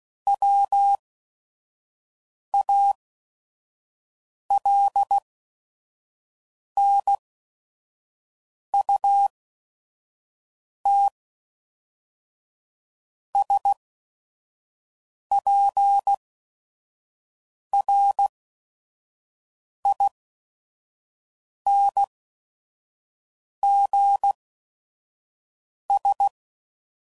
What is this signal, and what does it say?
Morse code, heard as a keyed tone, about 16 wpm, spaced as for 5 wpm, 800 Hz. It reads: WALNUTSPRINGS
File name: CW Code Groups @5WPM / fromSM's / Cities